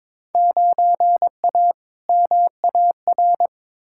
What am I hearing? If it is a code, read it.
Morse code, heard as a keyed tone, about 22 wpm, 700 Hz. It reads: 9A MAR
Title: Field Day Exchanges Encore - 22wpm